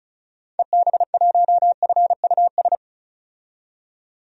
Morse code, heard as a keyed tone, about 36 wpm, 700 Hz. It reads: EB1FUS
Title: DXCC-100 Call Signs - 6 Characters - 35wpm